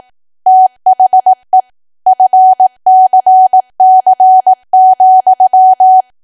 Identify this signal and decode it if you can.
Morse code, heard as a keyed tone, about 18 wpm, 750 Hz. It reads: THE FCC,
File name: cwa / W1AW-Level2 / 18WPM